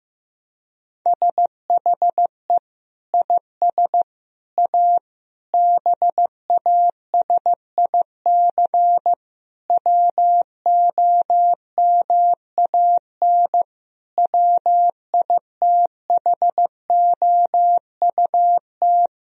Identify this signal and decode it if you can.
Morse code, heard as a keyed tone, about 16 wpm, 700 Hz. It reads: SHE IS A BASIC WOMAN WITHOUT
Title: Sentences from Top 1000 Words - Review Part 3 - 15wpm